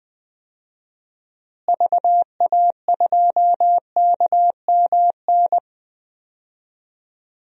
Morse code, heard as a keyed tone, about 20 wpm, 700 Hz. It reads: VA2KMN
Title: DXCC-100 Call Signs - 6 Characters - Repeated 3x - 20wpm